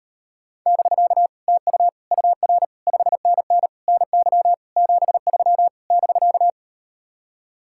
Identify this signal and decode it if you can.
Morse code, heard as a keyed tone, about 38 wpm, 700 Hz. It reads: <BK> TU UR 5NN NY 73 <BK>